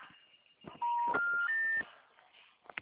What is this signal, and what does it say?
Special information tone.